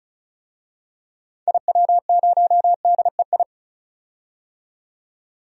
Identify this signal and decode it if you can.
Morse code, heard as a keyed tone, about 36 wpm, 700 Hz. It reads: IW0DEI
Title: DXCC-100 Call Signs - 6 Characters - 35wpm